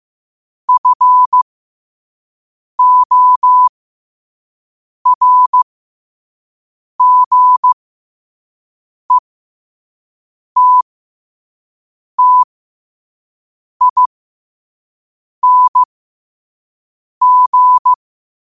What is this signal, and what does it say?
Morse code, heard as a keyed tone, about 15 wpm, spaced as for 5 wpm, 1000 Hz. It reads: FORGETTING
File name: qotdfiles / 5wpm